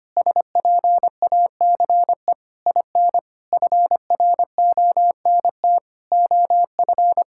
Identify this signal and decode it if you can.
Morse code, heard as a keyed tone, about 25 wpm, 700 Hz. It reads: SPACE IN FRONT OF
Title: Sentences from Top 1000 Words - Review Part 5 - 25wpm